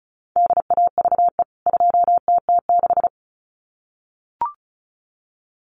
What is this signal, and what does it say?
Morse code, heard as a keyed tone, about 34 wpm, 700 Hz. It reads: DAVE 2TT6
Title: CWT Exchange 35wpm